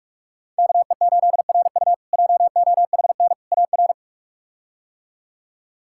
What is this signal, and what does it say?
Morse code, heard as a keyed tone, about 46 wpm, 700 Hz. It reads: KE8RU JOHN AR
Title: K1USN SST QSO Practice - 45wpm